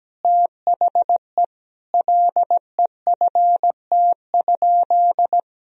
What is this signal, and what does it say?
Morse code, heard as a keyed tone, about 17 wpm, 700 Hz. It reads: THE LEFT?
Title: Sentences from Top 1000 Words - Review Part 2 - 17wpm